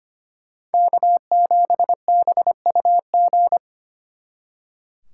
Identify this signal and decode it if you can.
Morse code, heard as a keyed tone, about 31 wpm, 700 Hz. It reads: K7BUG
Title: Speed Racing - USA 1x3 Call Signs 25wpm